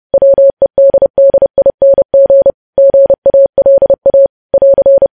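Morse code, heard as a keyed tone, about 30 wpm, 550 Hz. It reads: WEDDING GALA <AR>
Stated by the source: Morse news headlines at 30 wpm.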